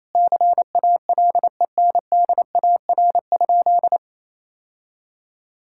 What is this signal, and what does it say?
Morse code, heard as a keyed tone, about 28 wpm, 700 Hz. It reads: CALENDAR?
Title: Sentences from Top 1600 Words 28wpm